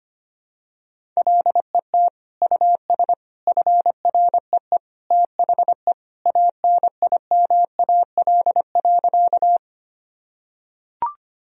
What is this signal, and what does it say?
Morse code, heard as a keyed tone, about 25 wpm, 700 Hz. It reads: LET US FREE THE ANIMAL.
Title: Sentences from Top 1000 Words - Review Part 1 - 25wpm